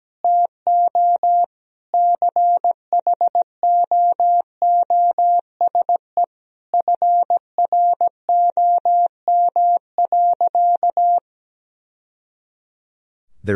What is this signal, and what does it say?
Morse code, heard as a keyed tone, about 17 wpm, 700 Hz. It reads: TO CHOOSE FROM.
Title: Sentences from Top 1000 Words - Review Part 4 - 17wpm